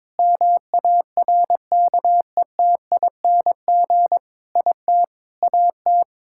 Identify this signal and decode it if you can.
Morse code, heard as a keyed tone, about 22 wpm, 700 Hz. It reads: MARKETING IT AT
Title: Sentences from Top 1000 Words - Review Part 5 - 22wpm